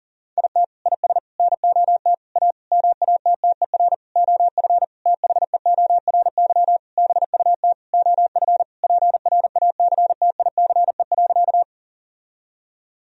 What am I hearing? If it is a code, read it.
Morse code, heard as a keyed tone, about 37 wpm, 700 Hz. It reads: IT IS NOT A MATTER OF THEORY BUT OF PRACTICE.
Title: Sentences from Top 1000 Words - Review Part 4 - 40wpm